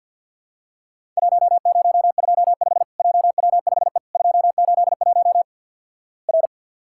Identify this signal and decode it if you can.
Morse code, heard as a keyed tone, about 50 wpm, 700 Hz. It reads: 1025 JW5E 281 R
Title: Worked All Europe DX Contest - QTC Exchange with Repeats - 50wpm